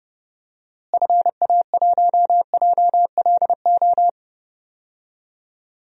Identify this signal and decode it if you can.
Morse code, heard as a keyed tone, about 31 wpm, 700 Hz. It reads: FA1JLO
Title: DXCC-100 Call Signs - 6 Characters - Repeated 2x - 30wpm